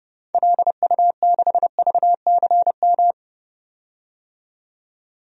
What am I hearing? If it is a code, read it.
Morse code, heard as a keyed tone, about 30 wpm, 700 Hz. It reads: LU6VCM
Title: DXCC-100 Call Signs - 6 Characters - 30wpm